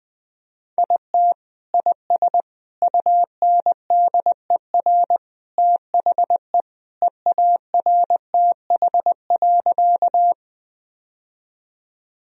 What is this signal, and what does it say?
Morse code, heard as a keyed tone, about 21 wpm, 700 Hz. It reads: IT IS UNDER THE EARTH.
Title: Sentences from Top 1000 Words - Review Part 1 - 20wpm